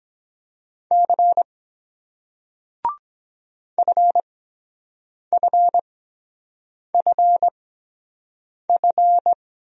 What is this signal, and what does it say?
Morse code, heard as a keyed tone, about 22 wpm, spaced as for 7 wpm, 700 Hz. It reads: C FFFF